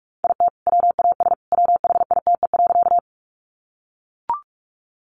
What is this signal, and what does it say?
Morse code, heard as a keyed tone, about 44 wpm, 700 Hz. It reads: IT WAS WHITE.
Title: Sentences from Top 300 Words 45wpm